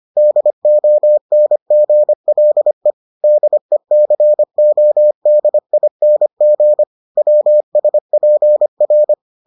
Morse code, heard as a keyed tone, about 25 wpm, 600 Hz. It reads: DONGLE DECODING WSPR